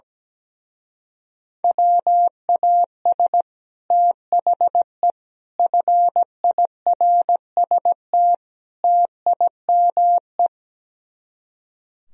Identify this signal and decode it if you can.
Morse code, heard as a keyed tone, about 17 wpm, 700 Hz. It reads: WAS THE FIRST TIME
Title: Speed-Racing - Sets of 4 Words 17wpm